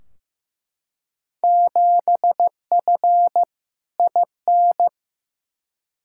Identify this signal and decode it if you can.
Morse code, heard as a keyed tone, about 15 wpm, 700 Hz. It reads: 7F IN
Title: Field Day Exchanges Encore - 15wpm